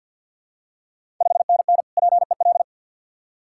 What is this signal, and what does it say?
Morse code, heard as a keyed tone, about 48 wpm, 700 Hz. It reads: 5NN PEL
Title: Ontario QSO Party Exchange 50wpm